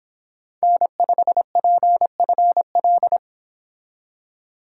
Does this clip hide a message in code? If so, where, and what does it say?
Yes: Morse code from 0.6 to 3.2 s, heard as a keyed tone, about 29 wpm, 700 Hz. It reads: N5PFL